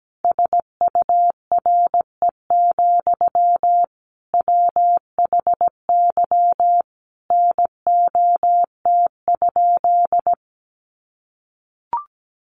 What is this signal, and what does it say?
Morse code, heard as a keyed tone, about 17 wpm, 700 Hz. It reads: SURE, WHY NOT?
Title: Sentences from Top 600 Words 17wpm